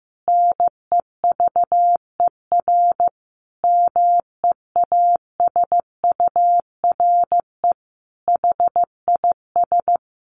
Morse code, heard as a keyed tone, about 15 wpm, 700 Hz. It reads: NEVER MEASURE HIS
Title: Sentences from Top 1200 Words 15wpm